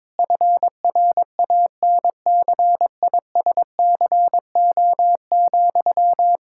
Morse code, heard as a keyed tone, about 21 wpm, 700 Hz. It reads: FRANCISCO,
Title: Practice QSO - Exchange 1 - Beginning - 22wpm